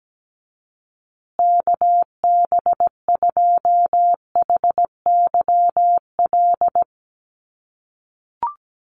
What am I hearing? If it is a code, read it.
Morse code, heard as a keyed tone, about 17 wpm, 700 Hz. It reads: KB2HYL
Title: USA Call Signs Repeated 2x 17wpm